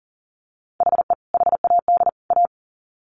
Morse code, heard as a keyed tone, about 40 wpm, 700 Hz. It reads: HE HAD A
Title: Sets of 3 Words 40wpm